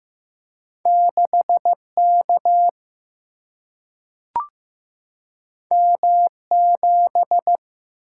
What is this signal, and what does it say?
Morse code, heard as a keyed tone, about 15 wpm, 700 Hz. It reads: 6 K M 7
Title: Single-Letter Single-Number 15wpm